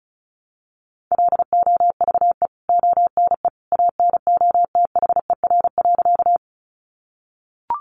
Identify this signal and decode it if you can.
Morse code, heard as a keyed tone, about 34 wpm, 700 Hz. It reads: LOVE ONE ANOTHER.
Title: Sentences from Top 600 Words 35wpm